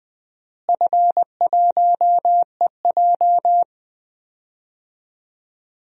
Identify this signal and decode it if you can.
Morse code, heard as a keyed tone, about 20 wpm, 700 Hz. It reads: F1EJ
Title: DXCC-100 Call Signs - 4 Characters - Repeated 2x - 20wpm